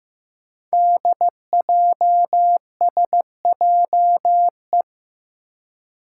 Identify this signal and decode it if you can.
Morse code, heard as a keyed tone, about 15 wpm, 700 Hz. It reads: DJSJE
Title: Five Challenging Letter-Number Combinations 15wpm